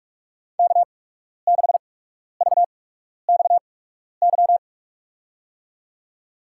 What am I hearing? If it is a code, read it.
Morse code, heard as a keyed tone, about 46 wpm, spaced as for 12 wpm, 700 Hz. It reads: K6VXY
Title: USA Call Signs - 5 Characters - 8x Character Spacing - 45wpm